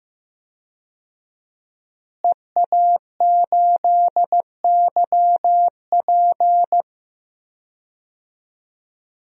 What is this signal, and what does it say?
Morse code, heard as a keyed tone, about 15 wpm, 700 Hz. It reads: EA8YP
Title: DXCC-100 Call Signs - 5 Characters - Repeated 3x - 15wpm